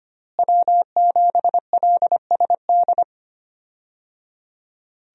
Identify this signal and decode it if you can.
Morse code, heard as a keyed tone, about 25 wpm, 700 Hz. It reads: W7LSD
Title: USA Call Signs - 5 Characters - Repeated 3x - 25wpm